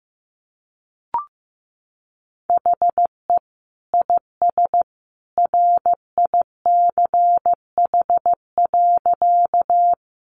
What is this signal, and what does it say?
Morse code, heard as a keyed tone, about 15 wpm, 700 Hz. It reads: HE IS RICH.
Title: Sentences from Top 1100 Words 15wpm